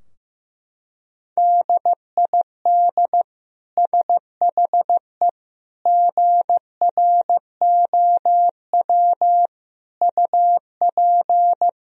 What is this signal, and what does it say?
Morse code, heard as a keyed tone, about 15 wpm, 700 Hz. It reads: DID SHE GROW UP
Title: Sentences from Top 1000 Words - Review Part 2 - 15wpm